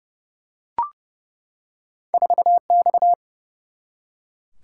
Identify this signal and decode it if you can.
Morse code, heard as a keyed tone, about 30 wpm, 700 Hz. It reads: 4 X
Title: Single-Letter Single-Number 30wpm